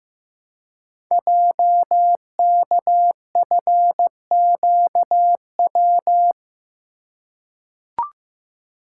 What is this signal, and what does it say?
Morse code, heard as a keyed tone, about 15 wpm, 700 Hz. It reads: JKFQW
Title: Code Groups - 5 Letters - 15wpm